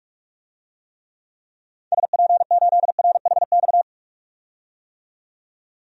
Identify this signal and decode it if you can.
Morse code, heard as a keyed tone, about 42 wpm, 700 Hz. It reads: SP8RHX
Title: DXCC-100 Call Signs - 6 Characters - 45wpm